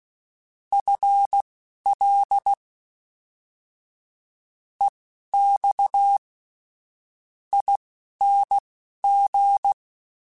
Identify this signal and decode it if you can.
Morse code, heard as a keyed tone, about 16 wpm, spaced as for 12 wpm, 800 Hz. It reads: FL EX ING
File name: FL / FL mp3s / Fast FL @10